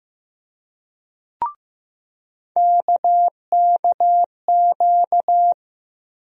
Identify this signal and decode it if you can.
Morse code, heard as a keyed tone, about 15 wpm, 700 Hz. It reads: KKQ
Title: USA Call Sign Postfixes - 3 Letters 15wpm